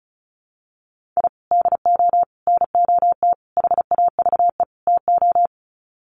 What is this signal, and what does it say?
Morse code, heard as a keyed tone, about 34 wpm, 700 Hz. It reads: I DO NOT HAVE TO
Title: Sets of 5 Words 35wpm